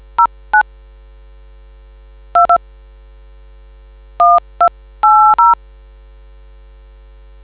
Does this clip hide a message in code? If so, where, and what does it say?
Yes: DTMF from 0.2 to 5.6 s, heard as telephone touch tones, tones of about 180 ms. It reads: 09 2 1280